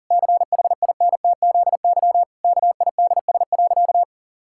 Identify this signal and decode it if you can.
Morse code, heard as a keyed tone, about 40 wpm, 700 Hz. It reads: CHINTZY KIDS.
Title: Pangram Challenge v2 - 40wpm